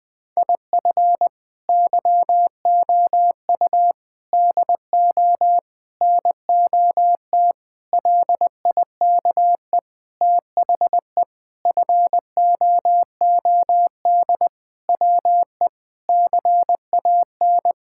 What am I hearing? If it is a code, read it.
Morse code, heard as a keyed tone, about 20 wpm, 700 Hz. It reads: IF YOU DO NOT LIKE THE FOOD WE CAN
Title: Sentences with 5-Word Phrases - Spoken Beginning - 20wpm